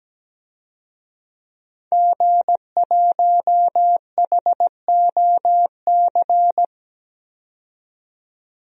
Morse code, heard as a keyed tone, about 17 wpm, 700 Hz. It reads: G1HOC
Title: DXCC-100 Call Signs - 5 Characters - 17wpm